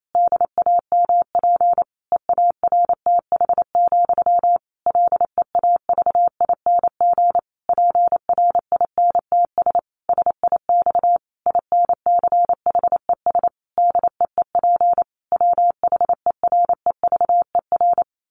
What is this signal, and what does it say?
Morse code, heard as a keyed tone, about 28 wpm, 700 Hz. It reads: DAMP EARTH, LEAVING PRINTS SIX INCHES DEEP WHEREVER
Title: A Sound Of Thunder 28wpm